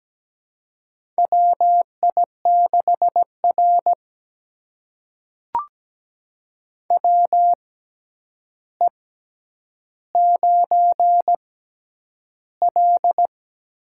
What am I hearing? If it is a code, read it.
Morse code, heard as a keyed tone, about 17 wpm, spaced as for 9 wpm, 700 Hz. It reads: WI6R WE9L
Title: USA Call Signs - 4 Characters - 6x Character Spacing - 17wpm